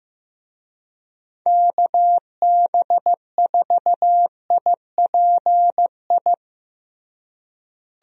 Morse code, heard as a keyed tone, about 15 wpm, 700 Hz. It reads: KB4IPI
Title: USA Call Signs - 6 Characters - Repeated 2x - 15wpm